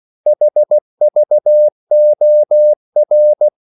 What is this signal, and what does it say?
Morse code, heard as a keyed tone, about 16 wpm, 600 Hz. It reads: HVOR
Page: Korte danske ord 16wpm